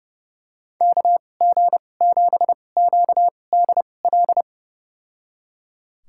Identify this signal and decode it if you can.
Morse code, heard as a keyed tone, about 32 wpm, spaced as for 22 wpm, 700 Hz. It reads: KG7QDL
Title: USA Call Signs - 6 Characters - 2x Character Spacing - 30wpm